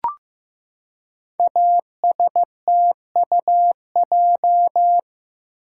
Morse code, heard as a keyed tone, about 15 wpm, 700 Hz. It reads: ASTUJ
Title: Code Groups - 5 Letters - 15wpm